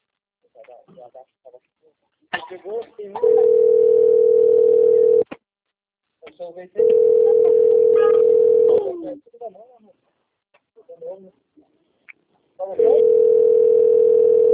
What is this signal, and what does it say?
Ringback tone (North America), 440 + 480 Hz, with speech.
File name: rg-1600-unknown- - - .64332